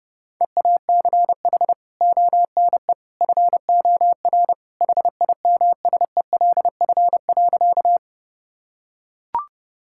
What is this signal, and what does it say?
Morse code, heard as a keyed tone, about 32 wpm, 700 Hz. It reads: EACH ONE FOR HIMSELF.
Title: Sentences from Top 1600 Words 30wpm